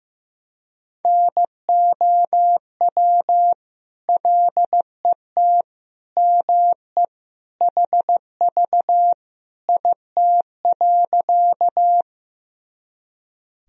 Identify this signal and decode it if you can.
Morse code, heard as a keyed tone, about 15 wpm, 700 Hz. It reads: NOW LET ME HV IT.
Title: Sentences from Top 300 Words and Abbreviations 15wpm